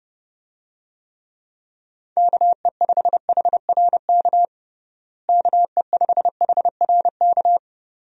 Morse code, heard as a keyed tone, about 30 wpm, 700 Hz. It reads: KE5HRK KE5HRK
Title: USA Call Signs - 6 Characters - Repeated 2x - 30wpm